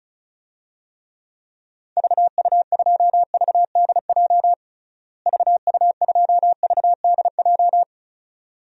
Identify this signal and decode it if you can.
Morse code, heard as a keyed tone, about 34 wpm, 700 Hz. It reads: VU2VDJ VU2VDJ
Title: DXCC-100 Call Signs - 6 Characters - Repeated 2x - 35wpm